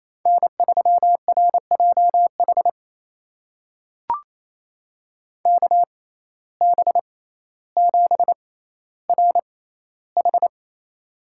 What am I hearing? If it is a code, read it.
Morse code, heard as a keyed tone, about 28 wpm, spaced as for 14 wpm, 700 Hz. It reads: N3RJH KB7RH
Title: USA Call Signs - 5 Characters - 6x Character Spacing - 28wpm